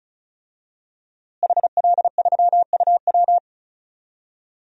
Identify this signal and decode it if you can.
Morse code, heard as a keyed tone, about 36 wpm, 700 Hz. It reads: HL3UW
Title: DXCC-100 Call Signs - 5 Characters - Repeated 2x - 35wpm